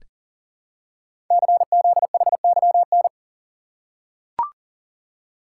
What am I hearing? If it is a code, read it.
Morse code, heard as a keyed tone, about 42 wpm, 690 Hz. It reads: CZHYN
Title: Code Groups - 5 Letters - 40wpm